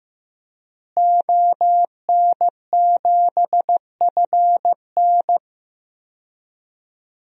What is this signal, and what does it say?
Morse code, heard as a keyed tone, about 15 wpm, 700 Hz. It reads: ON7FN
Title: DXCC-100 Call Signs - 5 Characters - Repeated 2x - 15wpm